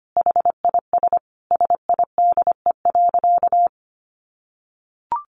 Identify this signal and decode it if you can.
Morse code, heard as a keyed tone, about 25 wpm, 700 Hz. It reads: HIS SIDE.
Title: Sentences from Top 400 Words 25wpm